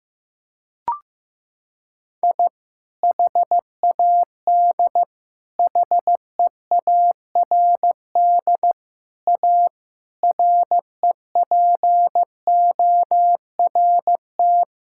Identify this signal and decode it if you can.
Morse code, heard as a keyed tone, about 15 wpm, 700 Hz. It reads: I HAD HEARD A REPORT
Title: Sentences from Top 1000 Words - Review Part 6 - 15wpm